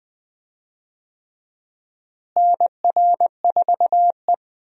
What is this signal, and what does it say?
Morse code, heard as a keyed tone, about 21 wpm, 700 Hz. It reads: NR4E
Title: USA Call Signs - 4 Characters - Repeated 3x - 20wpm